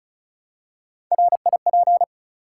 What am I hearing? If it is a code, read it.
Morse code, heard as a keyed tone, about 36 wpm, 700 Hz. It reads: RIP
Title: Three Letter Words - 35wpm